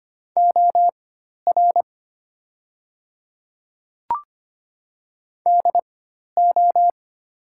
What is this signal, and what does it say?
Morse code, heard as a keyed tone, about 25 wpm, spaced as for 12 wpm, 700 Hz. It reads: OR DO
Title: 4x Character Spacing - Top 200 Words - 25wpm